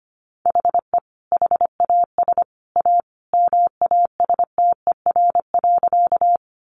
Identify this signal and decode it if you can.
Morse code, heard as a keyed tone, about 25 wpm, 700 Hz. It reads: HE HAS A MASTER.
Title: Sentences from Top 1300 Words 25wpm